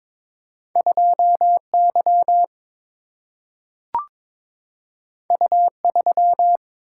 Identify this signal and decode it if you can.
Morse code, heard as a keyed tone, about 22 wpm, 700 Hz. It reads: 2 Y U 3
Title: Single-Letter Single-Number 22wpm